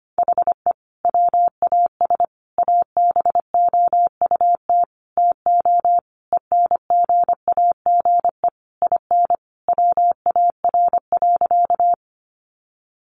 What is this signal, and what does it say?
Morse code, heard as a keyed tone, about 25 wpm, 700 Hz. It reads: HE WAS ABOUT TO ENGAGE IN WAR.
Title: Sentences from Top 1500 Words 25wpm